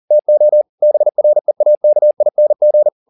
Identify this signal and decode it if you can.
Morse code, heard as a keyed tone, about 40 wpm, 600 Hz. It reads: TO BREAKING
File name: the old man and the sea 2 40wpm